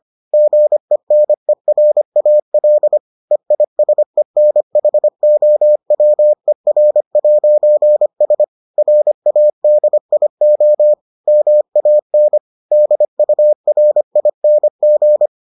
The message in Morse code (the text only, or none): GENERAL EISENHOWER'S RADIO MAN DURING